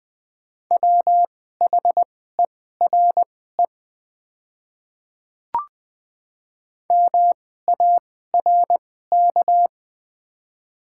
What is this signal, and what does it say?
Morse code, heard as a keyed tone, about 21 wpm, spaced as for 15 wpm, 700 Hz. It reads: WHERE MARK
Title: 2x Character Spacing - Top 300 Words - 20wpm